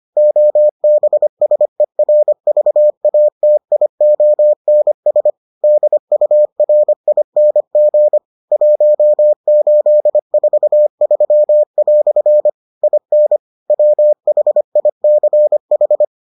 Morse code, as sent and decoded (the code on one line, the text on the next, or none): --- -... ... .-. ...- .- - .. --- -. ... / -.. ..- .-. .. -. --. / .---- ---.. ....- ...-- .-..-. / .. -. / .-- .... .. -.-. ....
OBSERVATIONS DURING 1843" IN WHICH